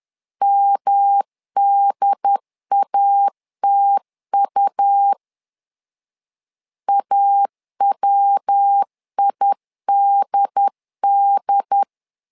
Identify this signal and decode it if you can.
Morse code, heard as a keyed tone, about 10 wpm, 800 Hz. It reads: MDATU AWIDD
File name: LEZIONI / 3 MWRO / 3 MW 10wpm 1959